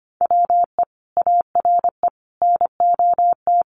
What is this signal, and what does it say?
Morse code, heard as a keyed tone, about 25 wpm, 700 Hz. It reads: WE ARE NOT
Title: Sets of 5 Words 25wpm